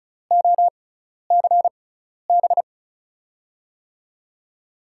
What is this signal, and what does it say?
Morse code, heard as a keyed tone, about 36 wpm, spaced as for 12 wpm, 700 Hz. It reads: OCB
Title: USA Call Signs - 6x Character Spacing - 35wpm